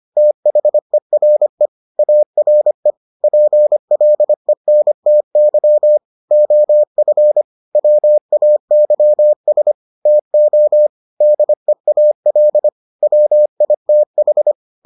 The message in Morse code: THERE ARE PLENTY OF WAYS TO DEAL WITH

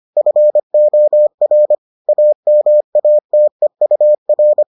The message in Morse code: FOR AMATEUR

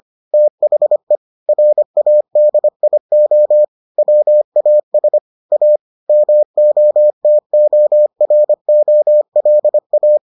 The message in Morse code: THE RADIO WAS A MOTOROLA

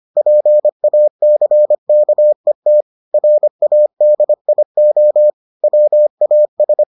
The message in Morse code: PACKET RADIO WAS

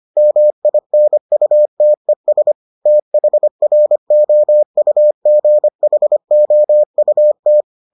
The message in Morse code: MINUTES THROUGHOUT

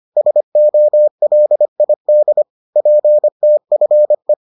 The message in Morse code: SOLID PTFE

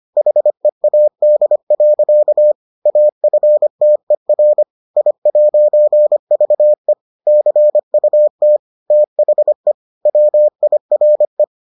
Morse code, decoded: HEAD. AFTER I'VE CUT THE WIRE

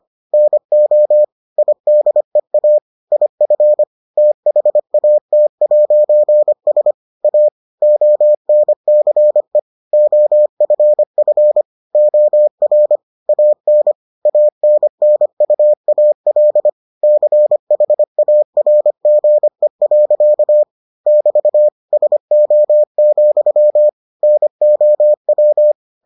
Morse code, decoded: NO IDEA IF THAT'S A ONCE OFF OR AN ANNUAL CHARGE. <BT> SO, NOW

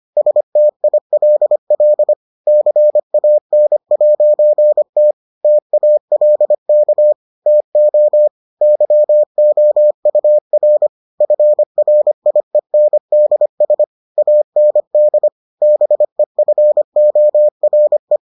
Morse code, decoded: STILL CAN'T TALK TO YOUR FRIENDS AND BEFORE

